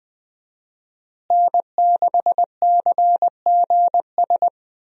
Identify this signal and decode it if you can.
Morse code, heard as a keyed tone, about 21 wpm, 700 Hz. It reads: N6CGS
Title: USA Call Signs - 5 Characters - Repeated 3x - 20wpm